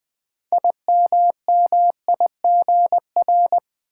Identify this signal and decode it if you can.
Morse code, heard as a keyed tone, about 20 wpm, 700 Hz. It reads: IMMIGR
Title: Build A Word - 3 Letters At A Time - 9-Letter Words - 20wpm